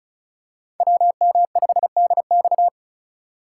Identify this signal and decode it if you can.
Morse code, heard as a keyed tone, about 34 wpm, 700 Hz. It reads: WM5DX
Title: K1USN SST QSO Practice - 35wpm